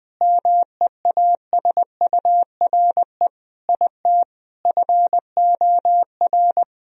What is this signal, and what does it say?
Morse code, heard as a keyed tone, about 21 wpm, 700 Hz. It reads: MEASURE IT FOR
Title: Sentences from Top 1000 Words - Review Part 2 - 20wpm